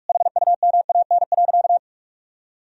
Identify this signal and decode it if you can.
Morse code, heard as a keyed tone, about 46 wpm, 700 Hz. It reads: HUMAN.
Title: Sentences from Top 1000 Words - Review Part 3 - 45wpm